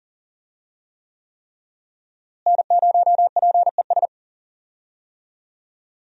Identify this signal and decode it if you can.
Morse code, heard as a keyed tone, about 40 wpm, 700 Hz. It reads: N0PES
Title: USA Call Signs - 5 Characters - Repeated 3x - 40wpm